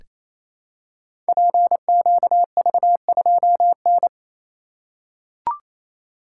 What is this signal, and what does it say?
Morse code, heard as a keyed tone, about 28 wpm, 700 Hz. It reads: PQV2N
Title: Code Groups - 5 Letter:Numbers - 28wpm